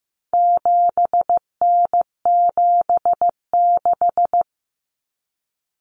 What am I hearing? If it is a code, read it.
Morse code, heard as a keyed tone, about 15 wpm, 700 Hz. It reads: 7N76
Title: Common Cut Numbers - 4-Digit 15wpm